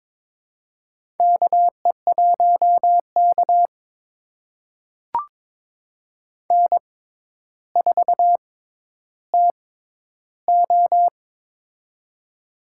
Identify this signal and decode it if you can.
Morse code, heard as a keyed tone, about 22 wpm, spaced as for 11 wpm, 700 Hz. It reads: KE1K N4TO
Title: USA Call Signs - 4 Characters - 6x Character Spacing - 22wpm